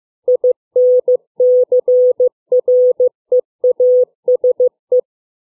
Morse code, heard as a keyed tone, about 15 wpm, 500 Hz. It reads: INCREASE
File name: CW / range of word lengths 3